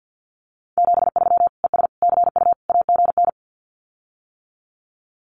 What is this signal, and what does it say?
Morse code, heard as a keyed tone, about 50 wpm, 700 Hz. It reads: 73 ES CU AGN
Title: Common QSO Phrases Encore 50wpm